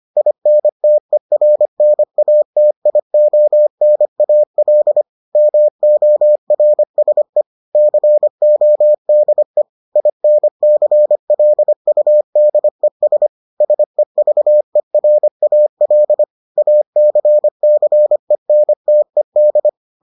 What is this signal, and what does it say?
Morse code, heard as a keyed tone, about 25 wpm, 600 Hz. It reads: INTERNATIONAL MORSE CODE INCLUDES SEVERAL ACCENTED